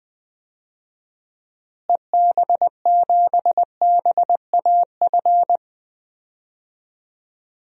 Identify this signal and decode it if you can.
Morse code, heard as a keyed tone, about 20 wpm, 700 Hz. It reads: EB7BAF
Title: DXCC-100 Call Signs - 6 Characters - Repeated 3x - 20wpm